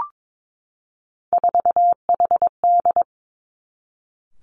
Morse code, heard as a keyed tone, about 22 wpm, 700 Hz. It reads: 4HD
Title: Three Challenging Letter-Number Combinations 22wpm